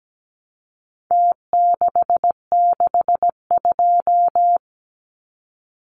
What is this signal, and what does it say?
Morse code, heard as a keyed tone, about 17 wpm, 700 Hz. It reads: T662
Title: Numbers - Contest Style - 4-Digit 17wpm